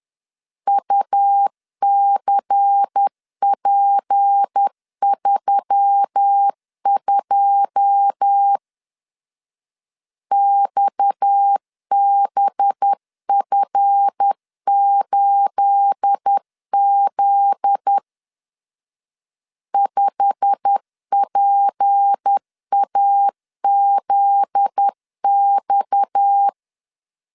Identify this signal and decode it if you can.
Morse code, heard as a keyed tone, about 10 wpm, 800 Hz. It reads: UCP32 XBF8Z 5PAZX
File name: CW / DET / 1 9 1500C S B 10wpm 4439